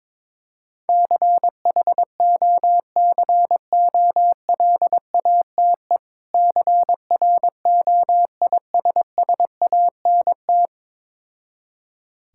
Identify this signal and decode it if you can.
Morse code, heard as a keyed tone, about 22 wpm, 700 Hz. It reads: CHOCOLATE CROISSANT
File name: Difficult Words with Hints - 22wpm